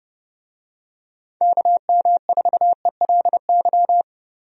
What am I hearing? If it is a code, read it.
Morse code, heard as a keyed tone, about 32 wpm, 700 Hz. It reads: KM4ELY
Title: USA Call Signs - 6 Characters - Repeated 2x - 30wpm